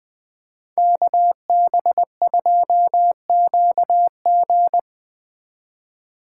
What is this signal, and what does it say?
Morse code, heard as a keyed tone, about 22 wpm, 700 Hz. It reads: KB2QG
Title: Speed Racing - USA 2x2 Call Signs 20wpm